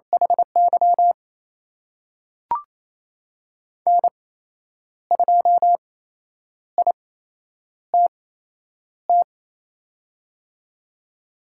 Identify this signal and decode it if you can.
Morse code, heard as a keyed tone, about 28 wpm, spaced as for 9 wpm, 700 Hz. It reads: HY N2ITT
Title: USA Call Signs - 8x Character Spacing - 28wpm